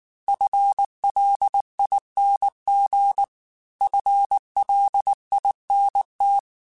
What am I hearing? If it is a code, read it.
Morse code, heard as a keyed tone, about 19 wpm, 800 Hz. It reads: FLING FLINT
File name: FL Runs / FL Runs mp3s / FL Runs @18